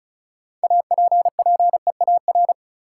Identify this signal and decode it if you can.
Morse code, heard as a keyed tone, about 36 wpm, 700 Hz. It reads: APPEAR
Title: Sentences from Top 1200 Words - New Word Drill Alt - 35wpm